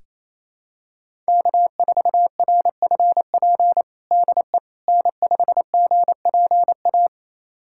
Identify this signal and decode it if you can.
Morse code, heard as a keyed tone, about 28 wpm, 700 Hz. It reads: K4RFP DE N5GPA
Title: USA Callsign DE Callsign - 5 Characters - 28wpm